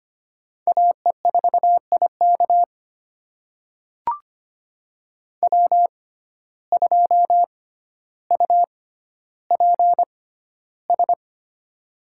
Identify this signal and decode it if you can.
Morse code, heard as a keyed tone, about 25 wpm, spaced as for 13 wpm, 700 Hz. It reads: AE4IK W2UPS
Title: USA Call Signs - 6x Character Spacing - 25wpm